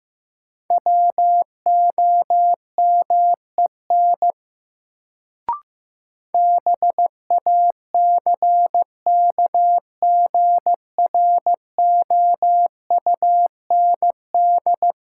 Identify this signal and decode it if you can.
Morse code, heard as a keyed tone, about 16 wpm, 700 Hz. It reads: WOMEN BACKGROUND
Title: Sentences from Top 1300 Words - New Word Drill - 15wpm